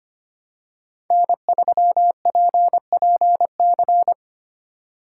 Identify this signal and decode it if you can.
Morse code, heard as a keyed tone, about 25 wpm, 700 Hz. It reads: N3PPC
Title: USA Call Signs - 3x Character Spacing - 25wpm